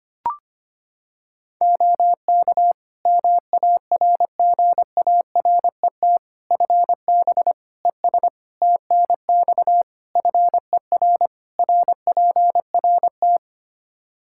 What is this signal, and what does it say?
Morse code, heard as a keyed tone, about 25 wpm, 700 Hz. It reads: OK MARGARET FB ES TNX FER RPRT
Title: Practice QSO - Exchange 2 - Info - 25wpm